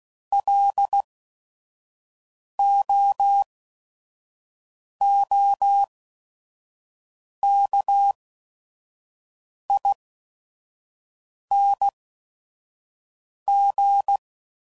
Morse code, heard as a keyed tone, about 16 wpm, spaced as for 5 wpm, 800 Hz. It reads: LOOKING